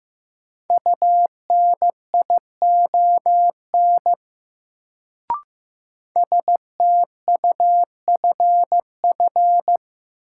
Morse code, heard as a keyed tone, about 15 wpm, 700 Hz. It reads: UNION STUFF
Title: Sentences from Top 1400 Words - New Word Drill Alt - 15wpm